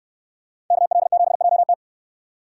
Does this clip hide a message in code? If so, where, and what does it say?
Yes: Morse code from 0.7 to 1.7 s, heard as a keyed tone, about 69 wpm, 700 Hz. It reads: KL7JT